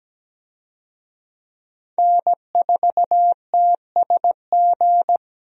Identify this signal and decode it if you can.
Morse code, heard as a keyed tone, about 17 wpm, 700 Hz. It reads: N4TSG
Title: USA Call Signs - 5 Characters - Repeated 3x - 17wpm